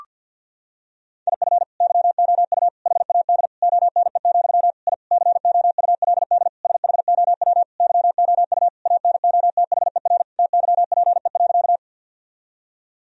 Morse code, heard as a keyed tone, about 51 wpm, 700 Hz. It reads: IF YOU HAD ONE, I COULD SHOW YOU ANOTHER TYPE.
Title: Sentences from Top 1000 Words - Review Part 3 - 50wpm